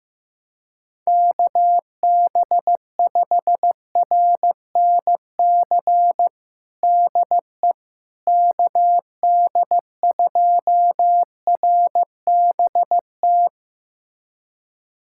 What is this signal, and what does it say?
Morse code, heard as a keyed tone, about 15 wpm, 700 Hz. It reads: KB5RNC DE KD2RBT
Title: USA Callsign DE Callsign - 6 Characters - 15wpm